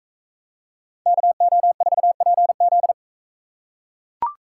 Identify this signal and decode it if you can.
Morse code, heard as a keyed tone, about 42 wpm, 700 Hz. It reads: KO4PZ